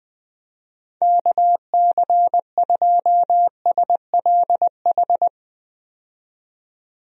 Morse code, heard as a keyed tone, about 20 wpm, 700 Hz. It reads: KC2SLH
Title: USA Call Signs - 6 Characters - 20wpm